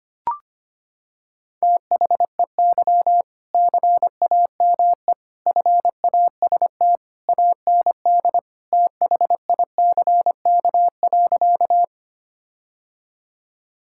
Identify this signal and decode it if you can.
Morse code, heard as a keyed tone, about 25 wpm, 700 Hz. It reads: THEY CAME FAST AND THICK.
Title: Sentences from Top 1600 Words 25wpm